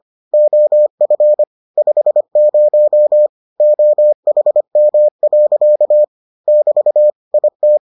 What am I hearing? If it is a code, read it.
Morse code, heard as a keyed tone, about 25 wpm, 600 Hz. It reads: OF 50 OHM. <BT> IT